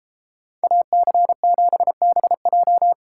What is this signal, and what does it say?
Morse code, heard as a keyed tone, about 33 wpm, 700 Hz. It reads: AC7BJ